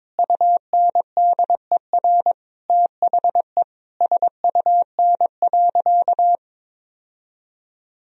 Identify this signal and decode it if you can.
Morse code, heard as a keyed tone, about 22 wpm, 700 Hz. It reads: UNDER THE SUN.
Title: Sentences from Top 1000 Words - Review Part 1 - 22wpm